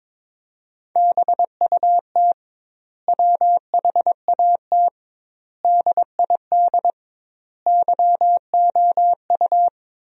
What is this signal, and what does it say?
Morse code, heard as a keyed tone, about 22 wpm, 700 Hz. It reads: BUT WHAT DID YOU
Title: 2x Word-Spacing - Sentences from Top 300 Words - 22wpm